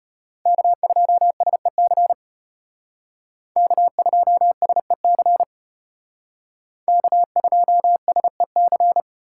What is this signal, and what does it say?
Morse code, heard as a keyed tone, about 31 wpm, 700 Hz. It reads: K2SEC K2SEC K2SEC
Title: Speed Racing - USA Call Signs 25wpm